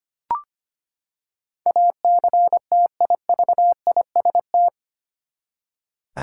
Morse code, heard as a keyed tone, about 25 wpm, 700 Hz. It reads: ACTIVIST
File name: Eight Letter Words - 25wpm